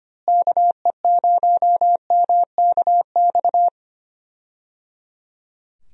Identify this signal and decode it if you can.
Morse code, heard as a keyed tone, about 25 wpm, 700 Hz. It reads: KE0MKX
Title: USA Call Signs - 6 Characters - Repeated 3x - 25wpm